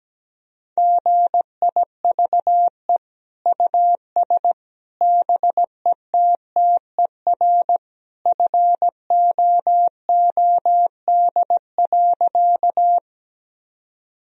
Morse code, heard as a keyed tone, about 17 wpm, 700 Hz. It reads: GIVE US BETTER FOOD.
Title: Sentences from Top 1000 Words - Review Part 1 - 17wpm